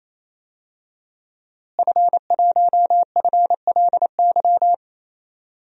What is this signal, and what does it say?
Morse code, heard as a keyed tone, about 28 wpm, 700 Hz. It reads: F1FLY
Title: DXCC-100 Call Signs - 5 Characters - Repeated 3x - 28wpm